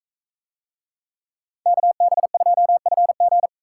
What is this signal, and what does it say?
Morse code, heard as a keyed tone, about 40 wpm, 700 Hz. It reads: KB2FG